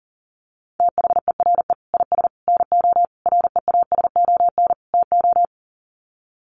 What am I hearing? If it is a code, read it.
Morse code, heard as a keyed tone, about 40 wpm, 700 Hz. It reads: THERE IS NO REASON TO
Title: Sets of 5 Words Encore 40wpm